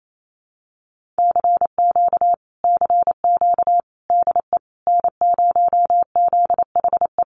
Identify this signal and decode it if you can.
Morse code, heard as a keyed tone, about 28 wpm, 700 Hz. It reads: CQ CQ DE N0ZHE
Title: Common QSO Phrases Encore 28wpm